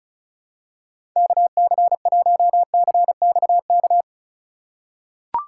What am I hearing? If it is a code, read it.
Morse code, heard as a keyed tone, about 34 wpm, 700 Hz. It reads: KC1CXK
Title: USA Call Signs - 6 Characters - 3x Character Spacing - 35wpm